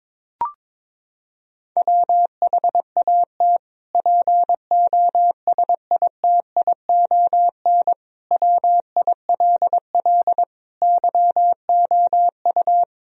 Morse code, heard as a keyed tone, about 22 wpm, 700 Hz. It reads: WHAT POSITION WILL YOU
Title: Sentences from Top 1000 Words - Review Part 3 - 22wpm